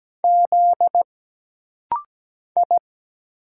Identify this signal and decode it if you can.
Morse code, heard as a keyed tone, about 17 wpm, 700 Hz. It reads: Z I
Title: Single Letter-Number Alt - Rapid-Fire 17wpm